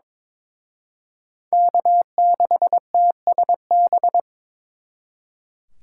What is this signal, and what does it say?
Morse code, heard as a keyed tone, about 26 wpm, 700 Hz. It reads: K6TSB
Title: Speed Racing - USA 1x3 Call Signs 22wpm